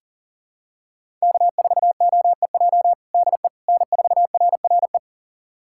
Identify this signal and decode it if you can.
Morse code, heard as a keyed tone, about 40 wpm, 700 Hz. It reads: K4OEJ DE N4RRE
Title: USA Callsign DE Callsign - 5 Characters - 40wpm